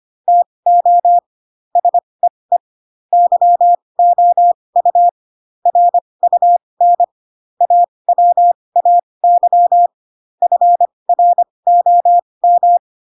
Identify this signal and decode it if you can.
Morse code, heard as a keyed tone, about 25 wpm, spaced as for 20 wpm, 700 Hz. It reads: TO SEE YOU RUN AWAY FROM
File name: Pinocchio / Pinocchio ch8 15 25 ik1hge